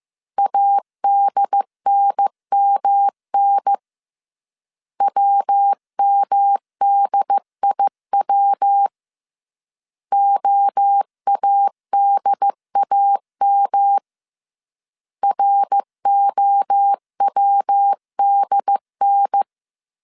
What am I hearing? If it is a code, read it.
Morse code, heard as a keyed tone, about 15 wpm, 800 Hz. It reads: ADNMN WMDIW OADAM ROWDN